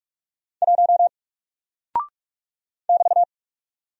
Morse code, heard as a keyed tone, about 48 wpm, 700 Hz. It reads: 1 <BT>
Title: Single Character (Letter:Number:Punctuation:Prosign) Alt 45wpm